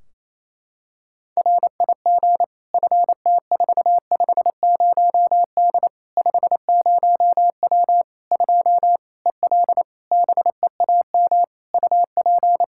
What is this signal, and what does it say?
Morse code, heard as a keyed tone, about 28 wpm, 700 Hz. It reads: RIG FT450D 50W 2 EL BEAM UP
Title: Practice QSO - Exchange 2 - Info - 28wpm